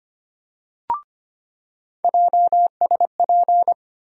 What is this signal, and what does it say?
Morse code, heard as a keyed tone, about 25 wpm, 700 Hz. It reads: JSP
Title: USA Call Sign Postfixes - 3 Letters 25wpm